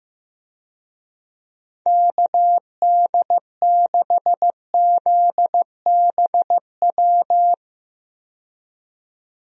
Morse code, heard as a keyed tone, about 15 wpm, 700 Hz. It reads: KD6ZBW
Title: USA Call Signs - 6 Characters - 15wpm